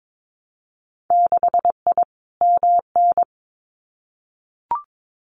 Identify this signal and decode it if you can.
Morse code, heard as a keyed tone, about 22 wpm, 700 Hz. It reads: 6I MN
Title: Winter Field Day Exchanges Encore 22wpm